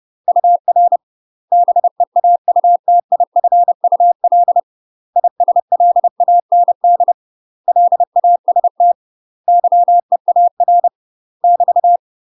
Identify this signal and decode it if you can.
Morse code, heard as a keyed tone, about 30 wpm, 700 Hz. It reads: UR BEAUTIFUL ISLAND LAST YEAR <BT>
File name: UK MASTER QSOs TRAINING / 30WPM / UK MASTER QSOs 0008